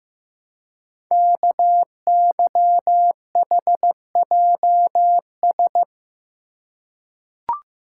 Morse code, heard as a keyed tone, about 15 wpm, 700 Hz. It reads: KYHJS
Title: Code Groups - 5 Letters - 15wpm